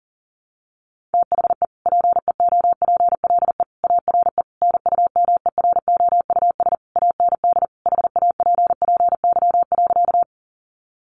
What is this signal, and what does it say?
Morse code, heard as a keyed tone, about 40 wpm, 700 Hz. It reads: THE PEOPLE ARE NUMEROUS AND HAPPY.
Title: Sentences from Top 1400 Words 40wpm